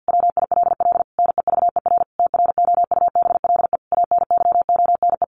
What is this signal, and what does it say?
Morse code, heard as a keyed tone, about 50 wpm, 700 Hz. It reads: WILL NEVER TROUBLE ANYONE
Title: Sentences from Top 1200 Words 50wpm